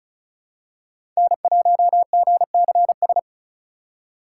Morse code, heard as a keyed tone, about 34 wpm, 700 Hz. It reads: N1GCS
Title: USA Call Signs - 5 Characters - 4x Character Spacing - 35wpm